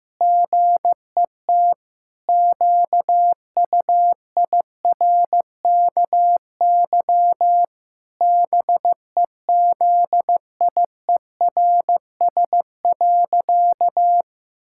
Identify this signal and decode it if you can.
Morse code, heard as a keyed tone, about 16 wpm, 700 Hz. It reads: GET QUIRKY BEZIERS.
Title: Pangram Challenge v2 - 15wpm